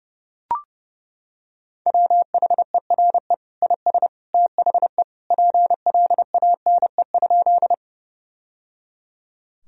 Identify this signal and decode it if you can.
Morse code, heard as a keyed tone, about 30 wpm, 700 Hz. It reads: WHERE IS THE PLANE?
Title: Sentences from Top 1000 Words - Review Part 3 - 30wpm